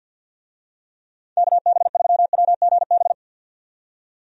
Morse code, heard as a keyed tone, about 51 wpm, 700 Hz. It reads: KB3WGB
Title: USA Call Signs - 6 Characters - 50wpm